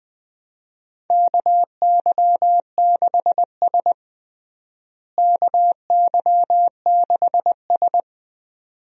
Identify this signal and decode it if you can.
Morse code, heard as a keyed tone, about 20 wpm, 700 Hz. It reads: KY6S KY6S
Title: USA Call Signs - 4 Characters - Repeated 2x - 20wpm